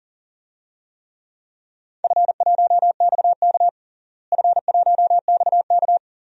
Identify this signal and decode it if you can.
Morse code, heard as a keyed tone, about 40 wpm, 700 Hz. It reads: F1XK F1XK
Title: DXCC-100 Call Signs - 4 Characters - Repeated 2x - 40wpm